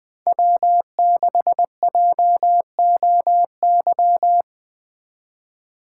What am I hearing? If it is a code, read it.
Morse code, heard as a keyed tone, about 20 wpm, 700 Hz. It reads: W6JOY
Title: Speed Racing - USA 1x3 Call Signs 17wpm